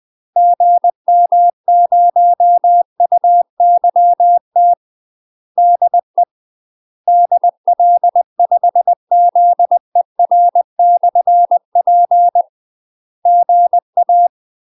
Morse code, heard as a keyed tone, about 20 wpm, 700 Hz. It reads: GM0UYT DE DL5ZER/P GA
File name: UK MASTER QSOs TRAINING/20WPM